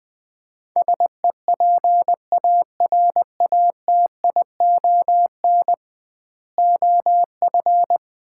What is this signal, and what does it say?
Morse code, heard as a keyed tone, about 21 wpm, 700 Hz. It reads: SEPARATION OF
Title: 2x Word-Spacing - Sets of 5 Words - 20wpm